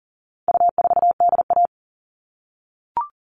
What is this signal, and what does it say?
Morse code, heard as a keyed tone, about 40 wpm, 700 Hz. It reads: U4DA
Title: All Cut Numbers - 4-Digit 40wpm